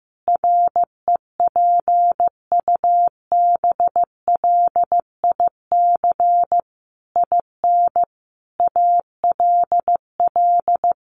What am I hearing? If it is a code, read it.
Morse code, heard as a keyed tone, about 15 wpm, 700 Hz. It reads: REPUBLIC IN ALL
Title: Sentences from Top 1500 Words 15wpm